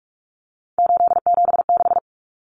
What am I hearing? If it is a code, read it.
Morse code, heard as a keyed tone, about 44 wpm, 700 Hz. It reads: 876
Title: Common Cut Numbers - 3-Digit 45wpm